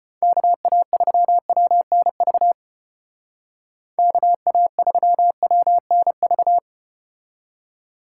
Morse code, heard as a keyed tone, about 34 wpm, 700 Hz. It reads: KA3WNV KA3WNV